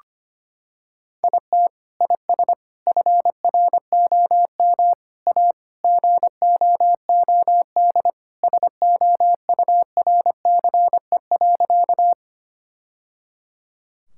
Morse code, heard as a keyed tone, about 25 wpm, 700 Hz. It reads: IT IS FROM A GOOD SOURCE.
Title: Sentences from Top 1000 Words - Review Part 5 - 25wpm